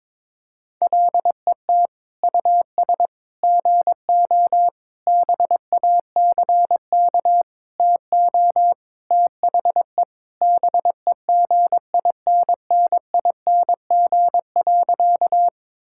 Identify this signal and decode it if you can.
Morse code, heard as a keyed tone, about 22 wpm, 700 Hz. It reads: LET US GO BACK TO THE BEGINNING.
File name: practice / 22wpm / Sentences with 4-Word Phrases - 22wpm